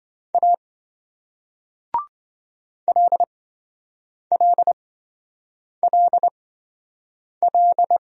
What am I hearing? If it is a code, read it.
Morse code, heard as a keyed tone, about 28 wpm, spaced as for 7 wpm, 700 Hz. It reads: A LLLL